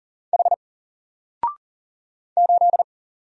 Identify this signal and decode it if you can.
Morse code, heard as a keyed tone, about 39 wpm, 700 Hz. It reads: H 8
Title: Single Letter-Number 40wpm